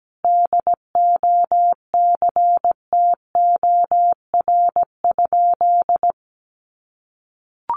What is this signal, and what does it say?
Morse code, heard as a keyed tone, about 17 wpm, 700 Hz. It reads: DOCTOR?
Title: Sentences from Top 1500 Words 17wpm